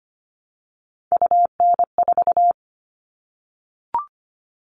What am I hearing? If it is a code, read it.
Morse code, heard as a keyed tone, about 25 wpm, 700 Hz. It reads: UN4
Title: All Cut Numbers - 3-Digit 25wpm